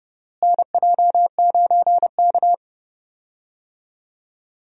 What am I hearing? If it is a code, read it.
Morse code, heard as a keyed tone, about 32 wpm, 700 Hz. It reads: NJ9K
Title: USA Call Signs - 4 Characters - Repeated 2x - 30wpm